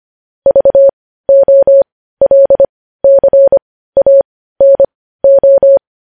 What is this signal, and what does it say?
Morse code, heard as a keyed tone, about 25 wpm, spaced as for 15 wpm, 550 Hz. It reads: VOLCANO
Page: News Headlines in Morse Code at 15 WPM